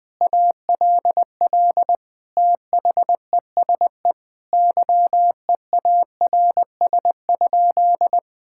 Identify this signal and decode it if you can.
Morse code, heard as a keyed tone, about 20 wpm, 700 Hz. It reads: ALL THESE YEARS?
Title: Sentences from Top 1000 Words - Review Part 6 - 20wpm